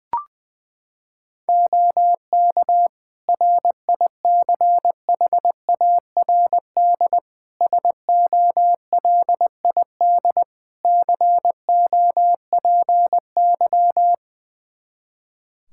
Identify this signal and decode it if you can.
Morse code, heard as a keyed tone, about 20 wpm, 700 Hz. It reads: OK RICHARD SOLID COPY
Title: Practice QSO - Exchange 3 - More Info - 20wpm